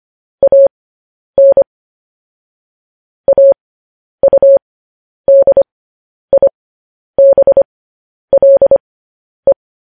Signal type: Morse code